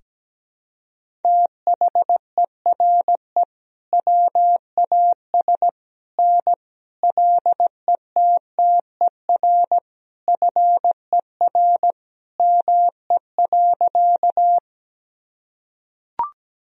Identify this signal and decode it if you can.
Morse code, heard as a keyed tone, about 17 wpm, 700 Hz. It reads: THERE WAS N LETTER FER ME.
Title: Sentences from Top 300 Words and Abbreviations 17wpm